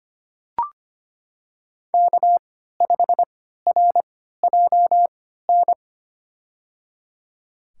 Morse code, heard as a keyed tone, about 25 wpm, spaced as for 14 wpm, 700 Hz. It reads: K5RJN
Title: USA Call Signs - 5 Characters - 3x Character Spacing - 25wpm